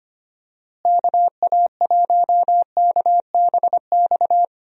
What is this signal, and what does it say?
Morse code, heard as a keyed tone, about 25 wpm, 700 Hz. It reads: KA1KBX
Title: USA Call Signs - 6 Characters - Repeated 2x - 25wpm